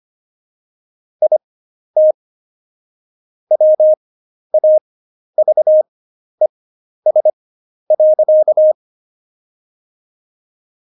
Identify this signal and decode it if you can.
Morse code, heard as a keyed tone, about 25 wpm, spaced as for 11 wpm, 620 Hz. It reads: IT WAVES.
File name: Jokes-25-char-10-eff